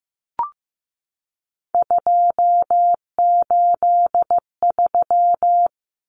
Morse code, heard as a keyed tone, about 15 wpm, 700 Hz. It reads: 283
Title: Numbers - 3-Digit 15wpm